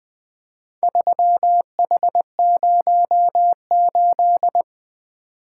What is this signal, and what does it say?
Morse code, heard as a keyed tone, about 20 wpm, 700 Hz. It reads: 3H08
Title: Speed-Racing - Four Letter-Number Combinations - 20wpm